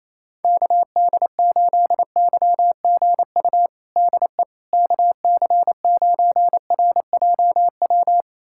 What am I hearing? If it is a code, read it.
Morse code, heard as a keyed tone, about 27 wpm, 700 Hz. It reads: KD8YGU DE KC9RJW
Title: USA Callsign DE Callsign - 6 Characters - 28wpm